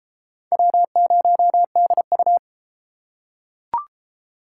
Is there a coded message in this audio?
Yes: Morse code from 0.5 to 2.4 s, heard as a keyed tone, about 29 wpm, 700 Hz. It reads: W0DU